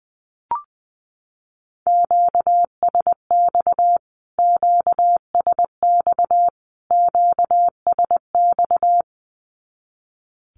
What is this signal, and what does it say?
Morse code, heard as a keyed tone, about 20 wpm, 700 Hz. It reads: QSX QSX QSX
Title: Q-Codes Repeated 3x 20wpm